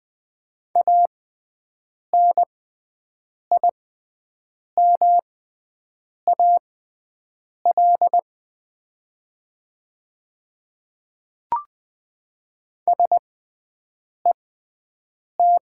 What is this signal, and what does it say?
Morse code, heard as a keyed tone, about 20 wpm, spaced as for 7 wpm, 700 Hz. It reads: ANIMAL SET
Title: 6x Character Spacing - Top 200 Words - 20wpm